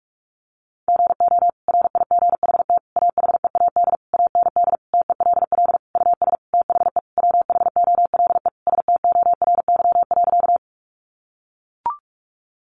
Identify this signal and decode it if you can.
Morse code, heard as a keyed tone, about 44 wpm, 700 Hz. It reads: GO RIGHT AHEAD AND TELL US THE WHOLE STORY.
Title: Sentences from Top 1100 Words 45wpm